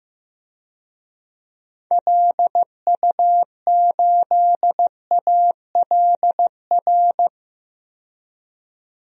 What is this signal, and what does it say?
Morse code, heard as a keyed tone, about 15 wpm, 700 Hz. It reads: LU8ALR
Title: DXCC-100 Call Signs - 6 Characters - 15wpm